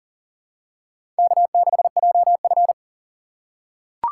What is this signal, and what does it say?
Morse code, heard as a keyed tone, about 43 wpm, 700 Hz. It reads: K6JF
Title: USA Call Signs - 4 Characters - 6x Character Spacing - 40wpm